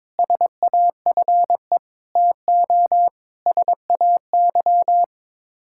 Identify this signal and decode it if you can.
Morse code, heard as a keyed tone, about 22 wpm, 700 Hz. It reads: SAFE TO SAY
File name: practice / 22wpm / Sets of 6 Words - 22wpm